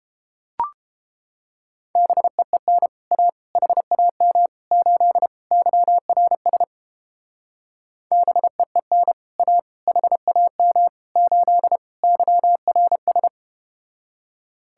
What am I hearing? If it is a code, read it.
Morse code, heard as a keyed tone, about 31 wpm, 700 Hz. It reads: BEEN A HAM 8 YRS BEEN A HAM 8 YRS